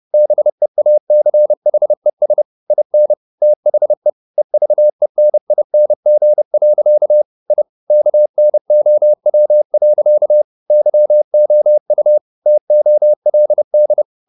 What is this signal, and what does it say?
Morse code, heard as a keyed tone, about 30 wpm, 600 Hz. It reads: BEACHES IN THE EVENING. I KNOW. YOU TOLD
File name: the old man and the sea / the old man and the sea 1 30wpm